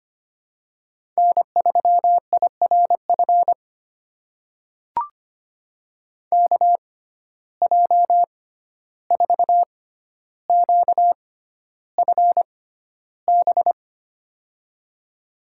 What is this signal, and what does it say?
Morse code, heard as a keyed tone, about 25 wpm, spaced as for 12 wpm, 700 Hz. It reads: N3IRF KJ4QFB
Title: USA Call Signs - 6x Character Spacing - 25wpm